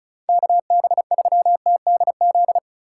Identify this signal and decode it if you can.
Morse code, heard as a keyed tone, about 36 wpm, 700 Hz. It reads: KB3TDZ
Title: USA Call Signs - 6 Characters - 3x Character Spacing - 35wpm